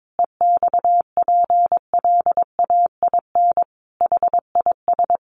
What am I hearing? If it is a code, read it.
Morse code, heard as a keyed tone, about 22 wpm, 700 Hz. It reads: EXPLAIN HIS
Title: Sentences from Top 1300 Words 22wpm